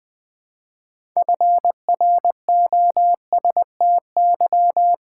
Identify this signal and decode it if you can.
Morse code, heard as a keyed tone, about 21 wpm, 700 Hz. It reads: FROSTY
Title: All-Age Appropriate Jokes - 20wpm